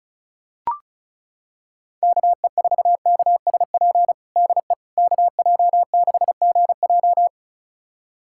Morse code, heard as a keyed tone, about 34 wpm, 690 Hz. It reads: KE4KSP DE KJ6GJ
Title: USA Callsign DE Callsign 35wpm